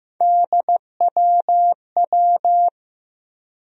Morse code, heard as a keyed tone, about 16 wpm, 700 Hz. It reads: DWW
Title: USA Call Sign Postfixes - 3 Letters 15wpm